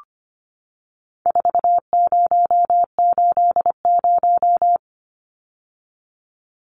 Morse code, heard as a keyed tone, about 25 wpm, 700 Hz. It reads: 4080
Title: Numbers - 4-Digit 25wpm